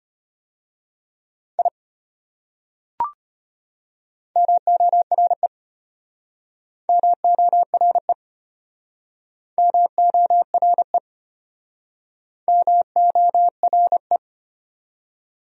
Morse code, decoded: I MORE MORE MORE MORE

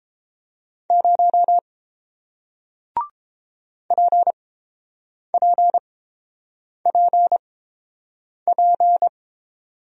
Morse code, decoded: 0 PPPP

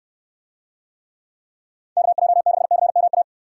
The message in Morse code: KJ6CMA

